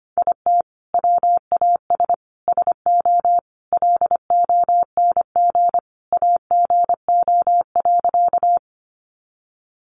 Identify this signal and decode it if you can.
Morse code, heard as a keyed tone, about 25 wpm, 700 Hz. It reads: IT WAS SO LONG AGO.
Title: Sentences from Top 500 Words 25wpm